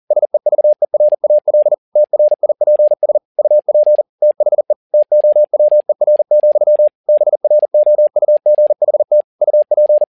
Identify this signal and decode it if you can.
Morse code, heard as a keyed tone, about 40 wpm, 600 Hz. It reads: SEVERAL TRIPS UP THE TOWER, BROUGHT UP
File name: learning center / tower trouble / tower touble 40wpm